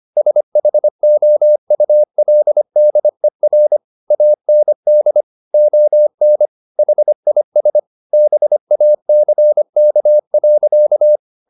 Morse code, heard as a keyed tone, about 25 wpm, 600 Hz. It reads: SHOULDER AND ON HIS BACK.
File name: the old man and the sea part 1 25wpm